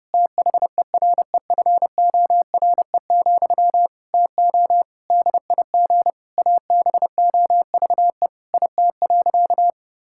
Morse code, heard as a keyed tone, about 30 wpm, 700 Hz. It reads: THEREFORE, TO DIG ABOVE IT.
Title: Sentences from Top 1600 Words 30wpm